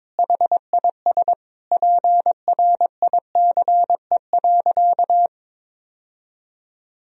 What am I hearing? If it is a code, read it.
Morse code, heard as a keyed tone, about 22 wpm, 700 Hz. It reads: HIS PRICE.
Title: Sentences from Top 1000 Words - Review Part 4 - 22wpm